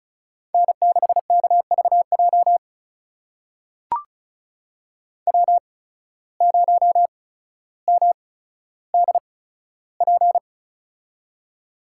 Morse code, heard as a keyed tone, about 36 wpm, spaced as for 15 wpm, 700 Hz. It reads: N6KVJ W0MDP
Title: USA Call Signs - 5 Characters - 8x Character Spacing - 35wpm